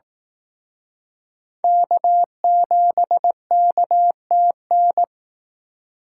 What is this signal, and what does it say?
Morse code, heard as a keyed tone, about 18 wpm, 700 Hz. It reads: K7KTN
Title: Speed Racing - USA Call Signs 15wpm